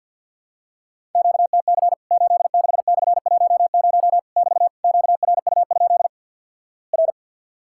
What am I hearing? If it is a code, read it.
Morse code, heard as a keyed tone, about 52 wpm, 700 Hz. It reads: QTC 86/10 <BT> QRU? R
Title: Worked All Europe DX Contest - QTC Exchange - 50wpm